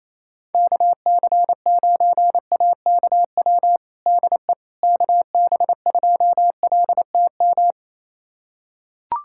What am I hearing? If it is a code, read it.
Morse code, heard as a keyed tone, about 28 wpm, 700 Hz. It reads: KC9AKW DE KB2LTM
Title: USA Callsign DE Callsign 28wpm